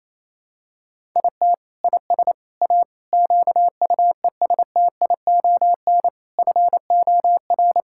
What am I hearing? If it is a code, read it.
Morse code, heard as a keyed tone, about 28 wpm, 700 Hz. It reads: IT IS A QUESTION FOR
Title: Sentences from Top 1000 Words - Review Part 6 - 28wpm